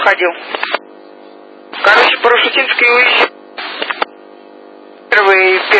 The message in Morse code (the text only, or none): none